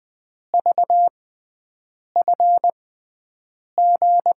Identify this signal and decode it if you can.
Morse code, heard as a keyed tone, about 20 wpm, spaced as for 7 wpm, 700 Hz. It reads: VFG
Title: USA Call Signs - 6x Character Spacing - 20wpm